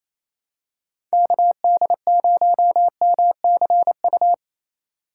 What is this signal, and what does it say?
Morse code, heard as a keyed tone, about 28 wpm, 700 Hz. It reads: KD0MCU
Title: USA Call Signs - 6 Characters - 3x Character Spacing - 28wpm